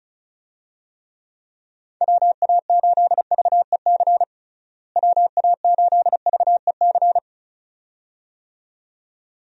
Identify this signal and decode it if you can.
Morse code, heard as a keyed tone, about 34 wpm, 700 Hz. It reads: WA8VEC WA8VEC
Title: USA Call Signs - 6 Characters - Repeated 2x - 35wpm